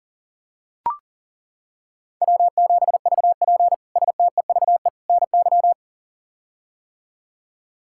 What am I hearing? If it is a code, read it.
Morse code, heard as a keyed tone, about 40 wpm, 700 Hz. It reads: W7VP STEVE NY
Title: K1USN SST QSO Practice - 40wpm